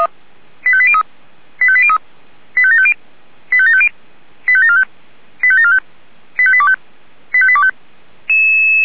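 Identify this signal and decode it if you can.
Five-tone selective call (ZVEI), heard as quick runs of five tones, tones of about 70 ms. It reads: 85803 85803 85750 85750 85745 85745 85825 85825
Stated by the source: ZVEI five-tone call 85827 received on an SDR.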